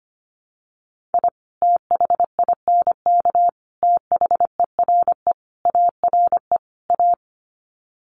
Morse code, heard as a keyed tone, about 25 wpm, 700 Hz. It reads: I THINK THERE ARE A
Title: Sets of 5 Words Encore2 25wpm